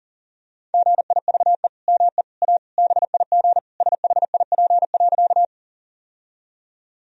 Morse code, heard as a keyed tone, about 40 wpm, 700 Hz. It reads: GIVE ME A BIG SHIP.
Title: Sentences from Top 1000 Words - Review Part 2 - 40wpm